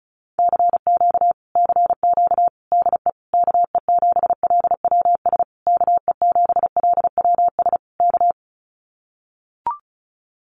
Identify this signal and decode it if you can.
Morse code, heard as a keyed tone, about 34 wpm, 700 Hz. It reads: CQ CQ DE KE7LWS KE7LWS K
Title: Common QSO Phrases Encore 35wpm